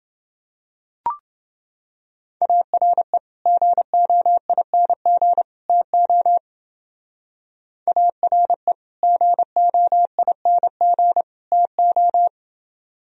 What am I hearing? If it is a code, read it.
Morse code, heard as a keyed tone, about 30 wpm, 700 Hz. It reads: ARE GOING TO ARE GOING TO